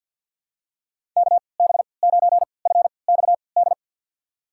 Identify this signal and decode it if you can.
Morse code, heard as a keyed tone, about 52 wpm, spaced as for 29 wpm, 700 Hz. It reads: KB9FXD
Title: USA Call Signs - 6 Characters - 3x Character Spacing - 50wpm